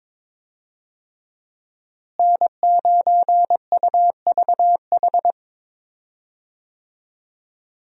Morse code, heard as a keyed tone, about 22 wpm, 700 Hz. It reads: N9UVH
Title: USA Call Signs - 5 Characters - Repeated 3x - 22wpm